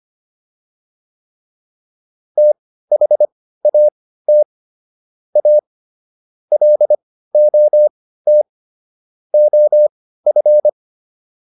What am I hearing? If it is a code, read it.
Morse code, heard as a keyed tone, about 25 wpm, spaced as for 15 wpm, 600 Hz. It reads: THAT A LOT OF